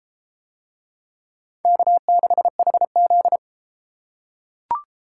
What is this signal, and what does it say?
Morse code, heard as a keyed tone, about 32 wpm, 700 Hz. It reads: K6HZ